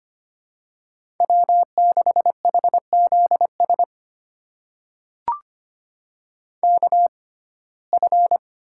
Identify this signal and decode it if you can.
Morse code, heard as a keyed tone, about 25 wpm, spaced as for 18 wpm, 700 Hz. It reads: W6HZS KF
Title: USA Call Signs - 6x Character Spacing - 25wpm